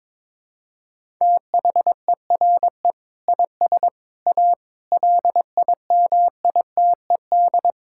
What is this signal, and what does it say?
Morse code, heard as a keyed tone, about 22 wpm, 700 Hz. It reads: THERE IS A LIMITED
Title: Sentences from Top 1000 Words - Review Part 6 - 22wpm